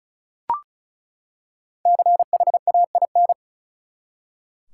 Morse code, heard as a keyed tone, about 36 wpm, 700 Hz. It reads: CHAIN
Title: Five Letter Words - 35wpm